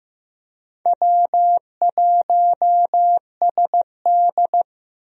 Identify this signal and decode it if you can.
Morse code, heard as a keyed tone, about 15 wpm, 700 Hz. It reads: W1SD
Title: USA Call Signs - 4 Characters - 15wpm